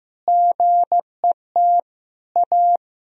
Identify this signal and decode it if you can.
Morse code, heard as a keyed tone, about 15 wpm, 700 Hz. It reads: GET A
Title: Sentences from Top 1000 Words - Review Part 3 - 15wpm